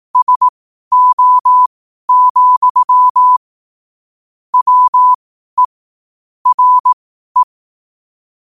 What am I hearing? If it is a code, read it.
Morse code, heard as a keyed tone, about 18 wpm, spaced as for 12 wpm, 1000 Hz. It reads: SO, WE RE
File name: 2026-01-07-12wpm